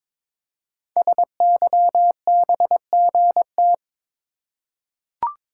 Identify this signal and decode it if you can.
Morse code, heard as a keyed tone, about 22 wpm, 700 Hz. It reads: SYBGT
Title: Code Groups - 5 Letters - 22wpm